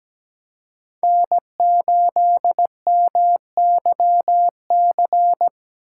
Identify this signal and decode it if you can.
Morse code, heard as a keyed tone, about 20 wpm, 700 Hz. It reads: N8MYC
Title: Speed Racing - USA Call Signs 17wpm